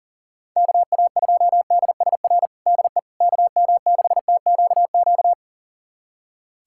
Morse code, heard as a keyed tone, about 40 wpm, 700 Hz. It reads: KA2DSR DE KM6TQQ
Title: USA Callsign DE Callsign 40wpm